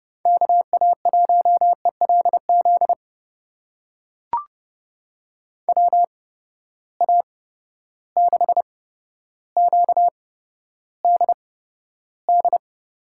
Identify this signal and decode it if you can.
Morse code, heard as a keyed tone, about 31 wpm, spaced as for 13 wpm, 700 Hz. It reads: KA1ELZ WA6QDD
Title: USA Call Signs - 6 Characters - 8x Character Spacing - 30wpm